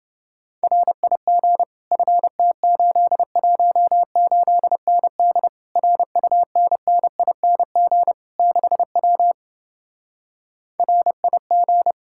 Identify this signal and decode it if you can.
Morse code, heard as a keyed tone, about 30 wpm, 700 Hz. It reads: RIG FT818ND RUNNING 6W RIG